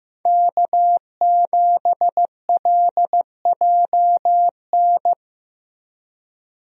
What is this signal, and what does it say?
Morse code, heard as a keyed tone, about 15 wpm, 700 Hz. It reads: K7LJN
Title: USA Call Signs - 5 Characters - Repeated 2x - 15wpm